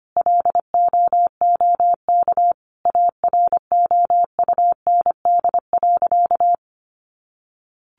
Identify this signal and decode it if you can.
Morse code, heard as a keyed tone, about 25 wpm, 700 Hz. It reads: LOOK AROUND.
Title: Sentences from Top 800 Words 25wpm